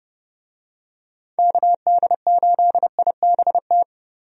Morse code, heard as a keyed tone, about 30 wpm, 700 Hz. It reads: KD8IBT
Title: USA Call Signs - 6 Characters - 30wpm